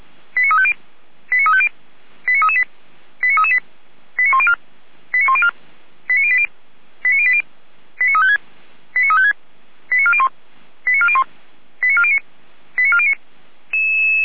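Five-tone selective call (ZVEI), heard as quick runs of five tones, tones of about 70 ms. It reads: 89350 89350 89338 89338 89104 89104 89080 89080 89356 89356 89401 89401 89409 89409
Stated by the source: ZVEI five-tone call received by SDR, archived by Feuerwehr Orlamünde fire brigade.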